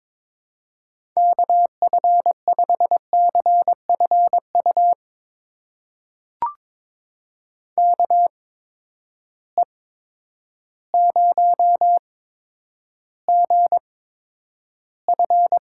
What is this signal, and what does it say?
Morse code, heard as a keyed tone, about 22 wpm, spaced as for 10 wpm, 700 Hz. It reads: KF5CFU KE0GF
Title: USA Call Signs - 6 Characters - 8x Character Spacing - 22wpm